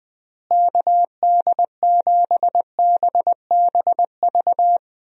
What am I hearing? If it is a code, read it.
Morse code, heard as a keyed tone, about 20 wpm, 700 Hz. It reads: KD7BBV
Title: USA Call Signs - 6 Characters - Repeated 2x - 20wpm